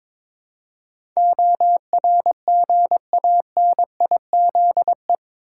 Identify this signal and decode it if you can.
Morse code, heard as a keyed tone, about 22 wpm, 700 Hz. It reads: ORGANIZE
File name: practice / 22wpm / Eight Letter Words - 22wpm